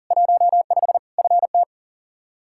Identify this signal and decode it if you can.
Morse code, heard as a keyed tone, about 40 wpm, 700 Hz. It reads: 15 FT
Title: Practice QSO - Exchange 2 - Info - 40wpm